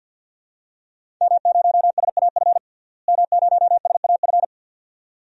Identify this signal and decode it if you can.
Morse code, heard as a keyed tone, about 52 wpm, 700 Hz. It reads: M0SAF M0SAF
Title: DXCC-100 Call Signs - 5 Characters - Repeated 2x - 50wpm